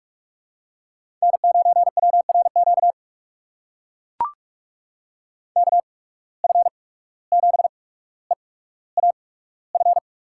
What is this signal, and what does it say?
Morse code, heard as a keyed tone, about 44 wpm, spaced as for 18 wpm, 700 Hz. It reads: N9WRQ KF7EAF
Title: USA Call Signs - 8x Character Spacing - 45wpm